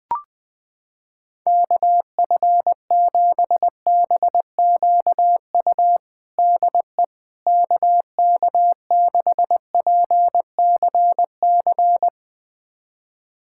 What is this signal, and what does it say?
Morse code, heard as a keyed tone, about 20 wpm, 700 Hz. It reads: KF7BQU DE KK6PCC
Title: USA Callsign DE Callsign 20wpm